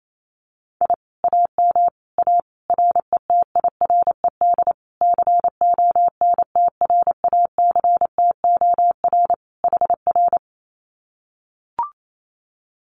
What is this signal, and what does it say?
Morse code, heard as a keyed tone, about 28 wpm, 700 Hz. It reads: I AM A RETIRED CONTRACTOR HR
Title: Common QSO Phrases Encore 28wpm